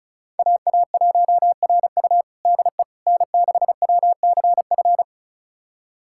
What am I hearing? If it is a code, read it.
Morse code, heard as a keyed tone, about 34 wpm, 700 Hz. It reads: AA1RU DE N6WCF
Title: USA Callsign DE Callsign - 5 Characters - 35wpm